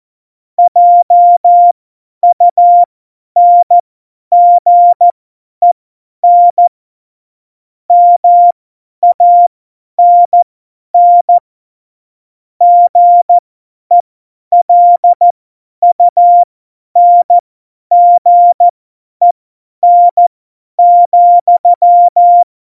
Morse code, heard as a keyed tone, about 14 wpm, spaced as for 10 wpm, 700 Hz. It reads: JUNGEN MANN GELUNGEN,
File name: Morsekurs / Witze / Witz 11 Juli 17, Tempo 50 F70